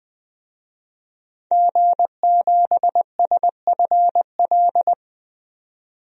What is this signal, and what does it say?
Morse code, heard as a keyed tone, about 20 wpm, 700 Hz. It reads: G7SFL
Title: DXCC-100 Call Signs - 5 Characters - 20wpm